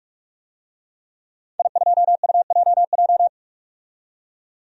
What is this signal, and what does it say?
Morse code, heard as a keyed tone, about 44 wpm, 700 Hz. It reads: I2UJJ
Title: DXCC-100 Call Signs - 5 Characters - 45wpm